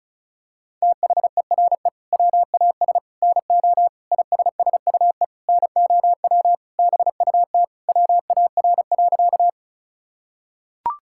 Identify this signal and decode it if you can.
Morse code, heard as a keyed tone, about 36 wpm, 700 Hz. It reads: THERE WAS NO ISSUE NOW BUT WAR.
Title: Sentences from Top 1000 Words - Review Part 6 - 35wpm